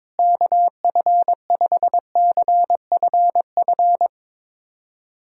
Morse code, heard as a keyed tone, about 22 wpm, 700 Hz. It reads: KF5CFF
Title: USA Callsign DE Callsign 22wpm